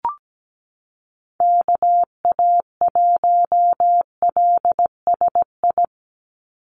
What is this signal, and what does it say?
Morse code, heard as a keyed tone, about 17 wpm, 700 Hz. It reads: KA1LSI
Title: USA Call Signs 17wpm